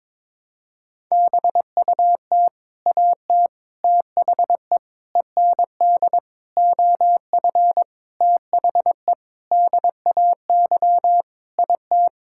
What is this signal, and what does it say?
Morse code, heard as a keyed tone, about 22 wpm, 700 Hz. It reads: BUT AT THE END OF THE DAY IT
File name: practice / 22wpm / Sentences with 8-Word Phrases - 22wpm